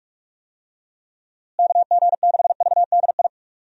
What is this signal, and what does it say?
Morse code, heard as a keyed tone, about 43 wpm, 700 Hz. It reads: KG6VDI
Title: USA Call Signs - 6 Characters - Repeated 3x - 45wpm